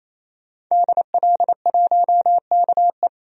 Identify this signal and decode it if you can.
Morse code, heard as a keyed tone, about 28 wpm, 700 Hz. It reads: DL1KE
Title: DXCC-100 Call Signs - 5 Characters - 28wpm